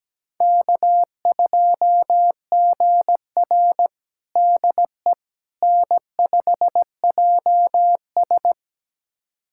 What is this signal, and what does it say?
Morse code, heard as a keyed tone, about 17 wpm, 700 Hz. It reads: K2GR DE N5JS
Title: USA Callsign DE Callsign - 4 Characters - 17wpm